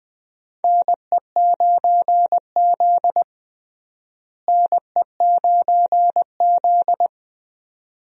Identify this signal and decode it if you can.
Morse code, heard as a keyed tone, about 20 wpm, 700 Hz. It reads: NE9Z NE9Z
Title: USA Call Signs - 4 Characters - Repeated 2x - 20wpm